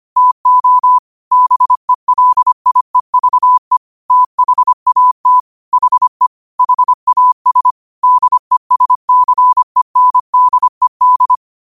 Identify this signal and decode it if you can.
Morse code, heard as a keyed tone, about 25 wpm, 1000 Hz. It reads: TO BELIEVE THAT HE HAS DESCENDED
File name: qotdfiles / 25wpm